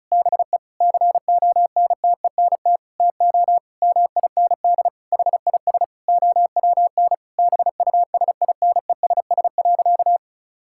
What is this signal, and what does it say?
Morse code, heard as a keyed tone, about 34 wpm, 700 Hz. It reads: BE CONTENT TO MIND HIS OWN BUSINESS.
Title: Sentences from Top 1000 Words - Review Part 6 - 35wpm